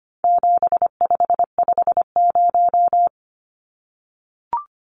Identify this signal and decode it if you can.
Morse code, heard as a keyed tone, about 25 wpm, 700 Hz. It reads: 7550
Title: Numbers - 4-Digit 25wpm